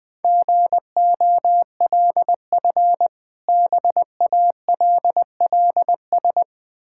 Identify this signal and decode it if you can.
Morse code, heard as a keyed tone, about 20 wpm, 690 Hz. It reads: GOLF BALLS
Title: Difficult Words with Hints - 20wpm